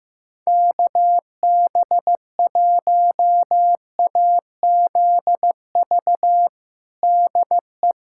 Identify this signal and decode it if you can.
Morse code, heard as a keyed tone, about 15 wpm, 700 Hz. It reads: KB1AZV DE
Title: USA Callsign DE Callsign - 6 Characters - 15wpm